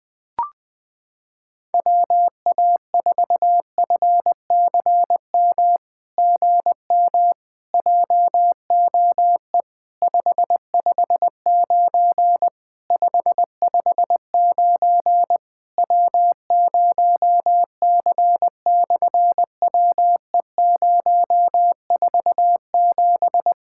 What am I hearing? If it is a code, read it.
Morse code, heard as a keyed tone, about 20 wpm, 700 Hz. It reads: WA4FCM GM JOE 559 559 W0C/WE047
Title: SOTA QSO Practice 20wpm